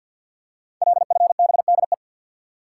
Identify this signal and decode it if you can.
Morse code, heard as a keyed tone, about 50 wpm, 700 Hz. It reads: LFBDE
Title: Code Groups - 5 Letters - 50wpm